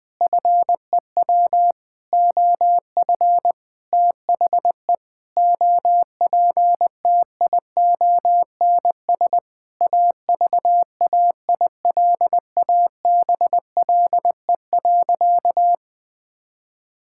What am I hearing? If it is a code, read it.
Morse code, heard as a keyed tone, about 20 wpm, 700 Hz. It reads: FEW OF THE OPTIONS AVAILABLE.
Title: Sentences with 7-Word Phrases - Spoken Beginning - 20wpm